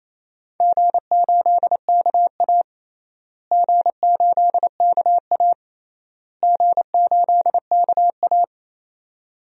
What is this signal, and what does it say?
Morse code, heard as a keyed tone, about 27 wpm, 700 Hz. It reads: G8KA G8KA G8KA
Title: DXCC-100 Call Signs - 4 Characters - Repeated 3x - 28wpm